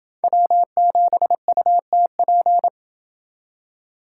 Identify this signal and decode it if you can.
Morse code, heard as a keyed tone, about 28 wpm, 700 Hz. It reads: W7UTP